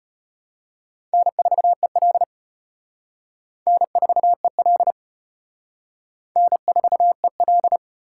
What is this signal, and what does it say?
Morse code, heard as a keyed tone, about 33 wpm, 700 Hz. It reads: N4EL N4EL N4EL